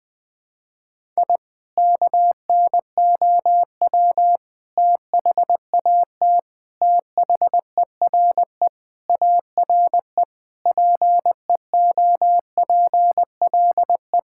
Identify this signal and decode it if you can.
Morse code, heard as a keyed tone, about 21 wpm, 700 Hz. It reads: I KNOW THAT THERE ARE PEOPLE
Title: Sentences with 3-Word Phrases - 20wpm